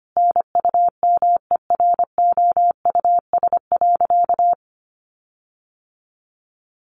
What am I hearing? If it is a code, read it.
Morse code, heard as a keyed tone, about 25 wpm, 700 Hz. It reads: NUMEROUS.
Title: Sentences from Top 1400 Words 25wpm